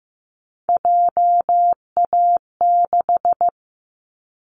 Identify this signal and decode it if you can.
Morse code, heard as a keyed tone, about 15 wpm, 700 Hz. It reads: JA6
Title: Callsign Prefixes for Top 50 DXCC - 15wpm